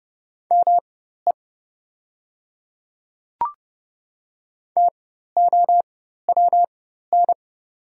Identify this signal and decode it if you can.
Morse code, heard as a keyed tone, about 30 wpm, spaced as for 14 wpm, 690 Hz. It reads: ME TOWN
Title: 4x Character Spacing - Top 500 Words - 30wpm